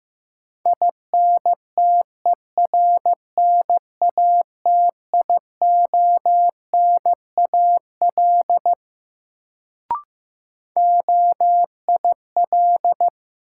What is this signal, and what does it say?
Morse code, heard as a keyed tone, about 15 wpm, 700 Hz. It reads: INTERNATIONAL OIL
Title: Sentences from Top 1000 Words - Review Part 4 - New Word Drill - 15wpm